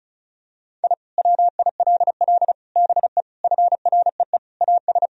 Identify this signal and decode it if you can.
Morse code, heard as a keyed tone, about 34 wpm, 700 Hz. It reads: I WILL BE FREE AS
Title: Sentences from Top 1000 Words - Review Part 1 - 35wpm